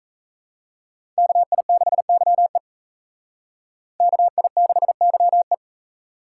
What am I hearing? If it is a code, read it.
Morse code, heard as a keyed tone, about 42 wpm, 700 Hz. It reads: KI6YE KI6YE